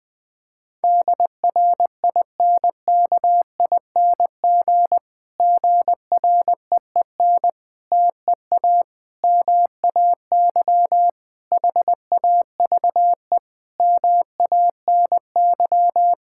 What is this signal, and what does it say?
Morse code, heard as a keyed tone, about 20 wpm, 700 Hz. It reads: DRINKING GREEN TEA MAY HAVE MANY
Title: Difficult Sentences with Hints - 20wpm